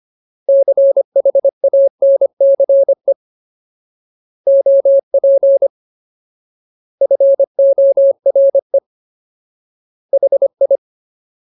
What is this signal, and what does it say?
Morse code, heard as a keyed tone, about 25 wpm, 550 Hz. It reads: CHANCE OP FORE HI
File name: Ultimate Word List / 25WPM Ultimate